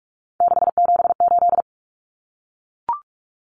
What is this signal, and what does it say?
Morse code, heard as a keyed tone, about 44 wpm, 700 Hz. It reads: 678
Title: Common Cut Numbers - 3-Digit 45wpm